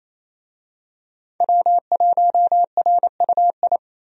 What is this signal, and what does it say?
Morse code, heard as a keyed tone, about 28 wpm, 700 Hz. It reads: W1RUI
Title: USA Call Signs - 5 Characters - Repeated 2x - 28wpm